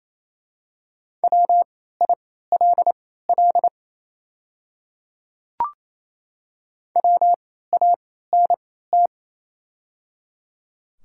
Morse code, heard as a keyed tone, about 28 wpm, spaced as for 16 wpm, 690 Hz. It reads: WILL WANT
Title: 3x Character Spacing - Top 300 Words - 28wpm